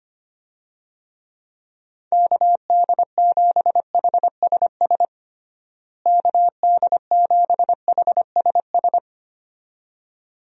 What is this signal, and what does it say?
Morse code, heard as a keyed tone, about 25 wpm, 700 Hz. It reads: KD7HSS KD7HSS
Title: USA Call Signs - 6 Characters - Repeated 2x - 25wpm